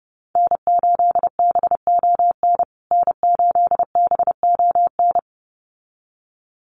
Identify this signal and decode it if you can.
Morse code, heard as a keyed tone, about 28 wpm, 700 Hz. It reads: N8BON N8BON
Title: USA Call Signs Repeated 2x 30wpm